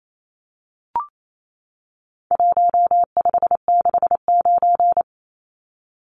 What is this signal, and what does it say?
Morse code, heard as a keyed tone, about 28 wpm, 700 Hz. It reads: 1569
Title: Numbers - Contest Style - 4-Digit 28wpm